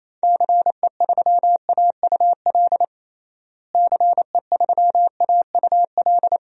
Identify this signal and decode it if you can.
Morse code, heard as a keyed tone, about 28 wpm, 700 Hz. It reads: CE3AUL CE3AUL
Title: DXCC-100 Call Signs - 6 Characters - Repeated 2x - 28wpm